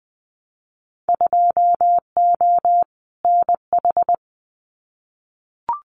Morse code, heard as a keyed tone, about 20 wpm, 700 Hz. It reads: 2O NH
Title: Winter Field Day Exchanges 20wpm